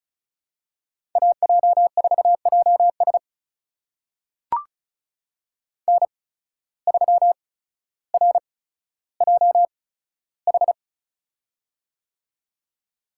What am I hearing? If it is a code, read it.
Morse code, heard as a keyed tone, about 34 wpm, spaced as for 15 wpm, 700 Hz. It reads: AJ4JS N3RJH
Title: USA Call Signs - 5 Characters - 8x Character Spacing - 35wpm